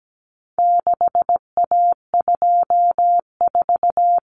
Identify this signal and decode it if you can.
Morse code, heard as a keyed tone, about 17 wpm, 700 Hz. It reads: 6A24
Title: Common Cut Numbers - 4-Digit 17wpm